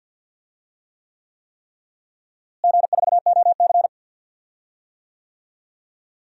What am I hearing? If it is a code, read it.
Morse code, heard as a keyed tone, about 51 wpm, 700 Hz. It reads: G4OC
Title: DXCC-100 Call Signs - 4 Characters - Repeated 3x - 50wpm